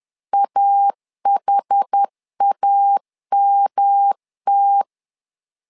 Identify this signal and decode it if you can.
Morse code, heard as a keyed tone, about 10 wpm, 800 Hz. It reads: AHAMT
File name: LEZIONI / 4 Q 10wpm 2210